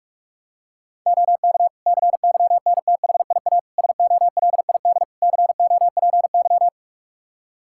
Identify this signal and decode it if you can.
Morse code, heard as a keyed tone, about 41 wpm, 700 Hz. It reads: OK CYNTHIA SOLID COPY
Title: Practice QSO - Exchange 3 - More Info - 45wpm